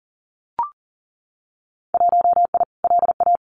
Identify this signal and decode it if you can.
Morse code, heard as a keyed tone, about 38 wpm, 700 Hz. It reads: 1I LA
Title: Winter Field Day Exchanges Encore 40wpm